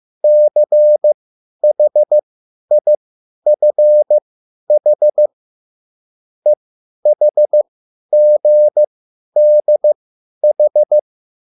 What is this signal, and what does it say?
Morse code, heard as a keyed tone, about 15 wpm, spaced as for 10 wpm, 600 Hz. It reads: CHIFH EHGDH